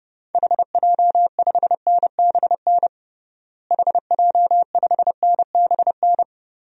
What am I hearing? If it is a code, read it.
Morse code, heard as a keyed tone, about 30 wpm, 700 Hz. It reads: HJ5NBN HJ5NBN
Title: DXCC-100 Call Signs - 6 Characters - Repeated 2x - 30wpm